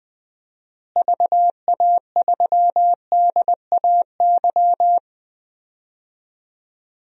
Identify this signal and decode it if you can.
Morse code, heard as a keyed tone, about 19 wpm, 700 Hz. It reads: VA3DAY
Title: DXCC-100 Call Signs - 6 Characters - Repeated 3x - 20wpm